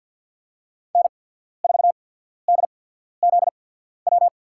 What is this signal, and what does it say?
Morse code, heard as a keyed tone, about 53 wpm, spaced as for 14 wpm, 700 Hz. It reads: N4DZW
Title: USA Call Signs - 5 Characters - 8x Character Spacing - 50wpm